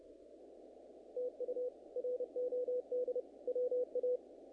Morse code, heard as a keyed tone, about 31 wpm, 500 Hz. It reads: TU ROD WA
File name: contests / naqp-cw-k5tr-op-k5pi- -2000